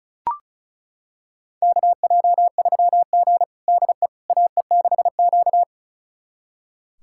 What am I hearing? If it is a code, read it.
Morse code, heard as a keyed tone, about 34 wpm, 700 Hz. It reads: KJ3G DE AE6Q
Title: USA Callsign DE Callsign - 4 Characters - 35wpm